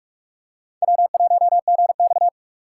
Morse code, heard as a keyed tone, about 44 wpm, 690 Hz. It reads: W1GX
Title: USA Call Signs - 4 Characters - 3x Character Spacing - 45wpm